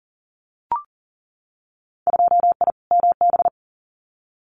Morse code, heard as a keyed tone, about 41 wpm, 700 Hz. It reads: 2I MB
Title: Winter Field Day Exchanges Encore 40wpm